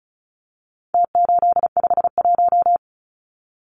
Morse code, heard as a keyed tone, about 34 wpm, 700 Hz. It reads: T851
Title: Numbers - Contest Style - 4-Digit 35wpm